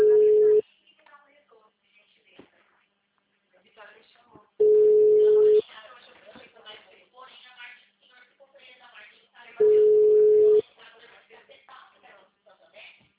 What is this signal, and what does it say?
Ringback tone (Europe, 425 Hz), with speech.
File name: out- -2010- - - .16